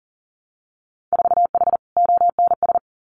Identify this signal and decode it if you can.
Morse code, heard as a keyed tone, about 40 wpm, 700 Hz. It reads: 4H ONS
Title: Winter Field Day Exchanges Encore 40wpm